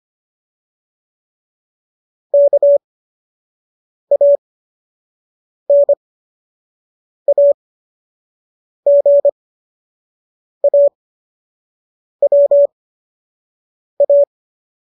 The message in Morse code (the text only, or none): KANAGAWA